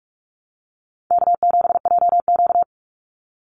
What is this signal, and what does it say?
Morse code, heard as a keyed tone, about 46 wpm, 700 Hz. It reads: K7JQ
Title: Easy Call Signs - 45wpm